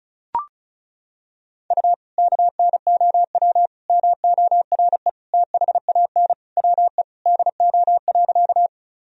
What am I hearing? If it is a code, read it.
Morse code, heard as a keyed tone, about 34 wpm, 700 Hz. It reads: U KNOW MORE THAN WE DO.
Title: Sentences from Top 100 Words and Abbreviations 35wpm